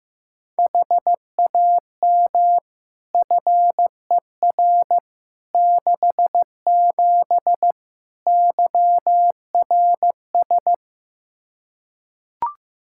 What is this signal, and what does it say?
Morse code, heard as a keyed tone, about 15 wpm, 700 Hz. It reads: HAM FER 67 YRS
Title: Practice QSO - Exchange 3 - More Info - 15wpm